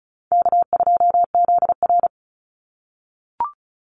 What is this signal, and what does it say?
Morse code, heard as a keyed tone, about 34 wpm, 700 Hz. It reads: K2ZR
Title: Call Signs - 35wpm